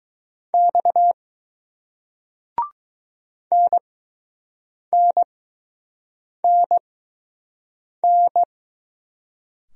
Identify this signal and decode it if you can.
Morse code, heard as a keyed tone, about 20 wpm, spaced as for 6 wpm, 700 Hz. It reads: X NNNN